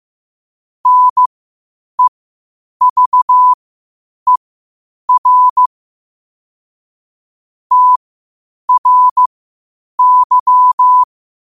Morse code, heard as a keyed tone, about 15 wpm, spaced as for 8 wpm, 1000 Hz. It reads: NEVER TRY
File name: qotdfiles / 8wpm